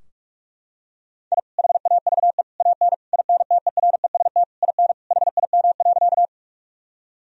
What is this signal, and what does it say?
Morse code, heard as a keyed tone, about 41 wpm, 700 Hz. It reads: I HAVE AN INTEREST IN HIM.
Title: Sentences from Top 1000 Words - Review Part 2 - 45wpm